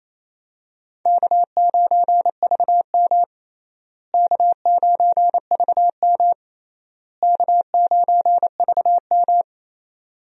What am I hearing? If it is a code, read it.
Morse code, heard as a keyed tone, about 28 wpm, 700 Hz. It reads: K9VM K9VM K9VM
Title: USA Call Signs - 4 Characters - Repeated 3x - 28wpm